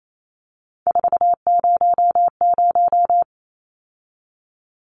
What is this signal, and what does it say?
Morse code, heard as a keyed tone, about 28 wpm, 700 Hz. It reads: 400
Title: Numbers - 3-Digit 28wpm